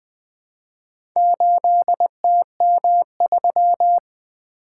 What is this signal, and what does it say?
Morse code, heard as a keyed tone, about 20 wpm, 700 Hz. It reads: 8TM3
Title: Four Letter-Number Combinations - 20wpm